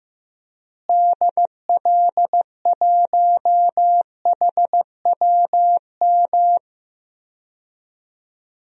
Morse code, heard as a keyed tone, about 15 wpm, 700 Hz. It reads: DL1HWM
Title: DXCC-100 Call Signs - 6 Characters - Repeated 3x - 15wpm